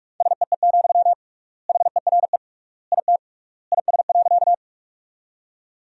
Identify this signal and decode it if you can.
Morse code, heard as a keyed tone, about 46 wpm, 700 Hz. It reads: SEE, HERE IT IS.
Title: 3x Word-Spacing - Sentences from Top 200 Words - 45wpm